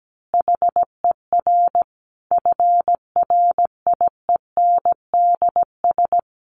Morse code, heard as a keyed tone, about 17 wpm, 700 Hz. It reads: HER FRIENDS
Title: Easy Phrases 17wpm